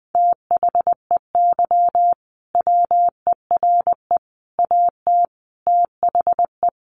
Morse code, heard as a keyed tone, about 20 wpm, 700 Hz. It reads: THEY WERE AT THE
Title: Sentences from Top 1200 Words 20wpm